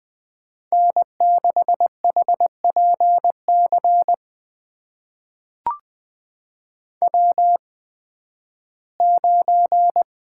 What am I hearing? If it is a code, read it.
Morse code, heard as a keyed tone, about 20 wpm, spaced as for 13 wpm, 700 Hz. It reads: N6HPC W9
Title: USA Call Signs - 5 Characters - 8x Character Spacing - 20wpm